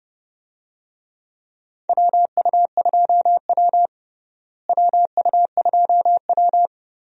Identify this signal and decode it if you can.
Morse code, heard as a keyed tone, about 30 wpm, 700 Hz. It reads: WU2W WU2W
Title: USA Call Signs - 4 Characters - Repeated 3x - 30wpm